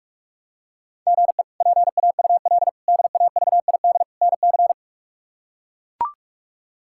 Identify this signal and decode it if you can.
Morse code, heard as a keyed tone, about 42 wpm, 700 Hz. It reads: GE PAUL DAVID NC
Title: K1USN SST QSO Practice - 45wpm